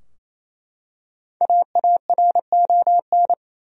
Morse code, heard as a keyed tone, about 27 wpm, 700 Hz. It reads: AARON
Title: USA Common Names - Max 5 Letters - 28wpm